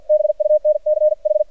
Morse code, heard as a keyed tone, about 55 wpm, spaced as for 46 wpm, 610 Hz, with radio noise. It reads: BUNCH
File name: archived / CRSnet